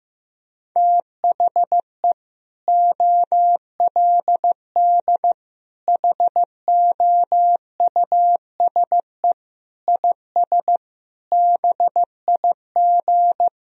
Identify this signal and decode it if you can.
Morse code, heard as a keyed tone, about 16 wpm, 700 Hz. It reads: THE OLD HOUSE IS BIG
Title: Sentences from Top 1000 Words - Review Part 1 - 15wpm